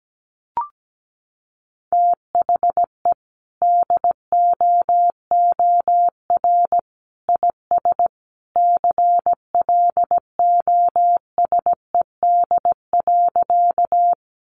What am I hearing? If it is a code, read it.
Morse code, heard as a keyed tone, about 17 wpm, 700 Hz. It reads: THE DOOR IS CLOSED.
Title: Sentences from Top 1400 Words 17wpm